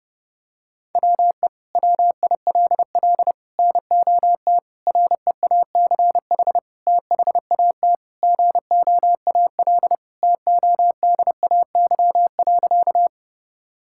Morse code, heard as a keyed tone, about 32 wpm, 700 Hz. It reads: WE WILL NOT REACH THAT GOAL TODAY.
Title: Sentences from Top 1000 Words - Review Part 5 - 30wpm